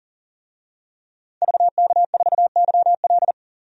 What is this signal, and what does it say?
Morse code, heard as a keyed tone, about 40 wpm, 700 Hz. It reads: VK4YL
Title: DXCC-100 Call Signs - 5 Characters - 40wpm